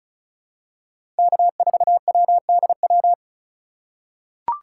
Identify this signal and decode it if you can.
Morse code, heard as a keyed tone, about 34 wpm, 700 Hz. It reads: K4WDW
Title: USA Call Signs - 5 Characters - 3x Character Spacing - 35wpm